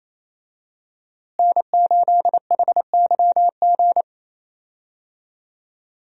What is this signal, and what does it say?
Morse code, heard as a keyed tone, about 28 wpm, 700 Hz. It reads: N8HYG
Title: USA Call Signs - 5 Characters - Repeated 2x - 28wpm